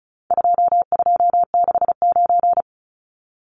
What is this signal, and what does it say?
Morse code, heard as a keyed tone, about 34 wpm, 700 Hz. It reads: 2269
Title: Numbers - 4-Digit 35wpm